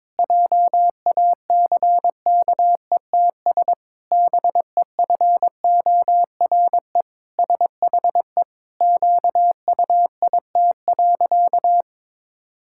Morse code, heard as a keyed tone, about 22 wpm, 700 Hz. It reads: JACKETS BEFORE SHE QUIT.
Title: Pangram Challenge v2 - 22wpm